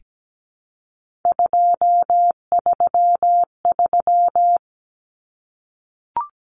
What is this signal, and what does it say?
Morse code, heard as a keyed tone, about 17 wpm, 700 Hz. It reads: 233
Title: Numbers - 3-Digit 17wpm